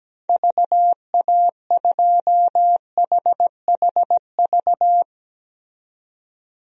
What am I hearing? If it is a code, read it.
Morse code, heard as a keyed tone, about 17 wpm, 700 Hz. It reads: VA2HHV
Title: DXCC-100 Call Signs - 6 Characters - 17wpm